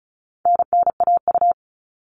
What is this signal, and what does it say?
Morse code, heard as a keyed tone, about 34 wpm, 700 Hz. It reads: NNAU
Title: All Cut Numbers - 4-Digit 35wpm